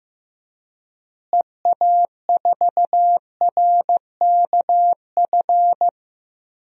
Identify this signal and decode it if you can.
Morse code, heard as a keyed tone, about 15 wpm, 700 Hz. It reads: EA4RKF
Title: DXCC-100 Call Signs - 6 Characters - Repeated 3x - 15wpm